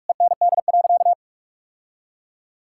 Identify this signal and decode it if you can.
Morse code, heard as a keyed tone, about 46 wpm, 700 Hz. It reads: END.
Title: Sentences from Top 1000 Words - Review Part 6 - 45wpm